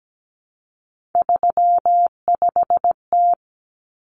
Morse code, heard as a keyed tone, about 17 wpm, 700 Hz. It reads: 35T
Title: Common Cut Numbers - 3-Digit 17wpm